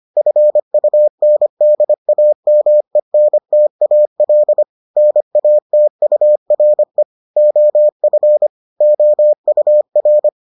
Morse code, heard as a keyed tone, about 25 wpm, 600 Hz. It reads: FUNDAMENTAL NATURE OF OUR